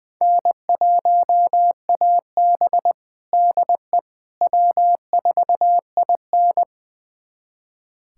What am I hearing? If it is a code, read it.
Morse code, heard as a keyed tone, about 21 wpm, 700 Hz. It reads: N1AB DE W4IN
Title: USA Callsign DE Callsign - 4 Characters - 20wpm